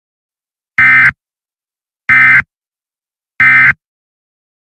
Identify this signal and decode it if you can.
Emergency Alert System (SAME) tones, heard as harsh buzzing bursts. It is heard from 0.8 to 3.7 s.
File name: Emergency Alert System Eas Tones Attention Signals High Quality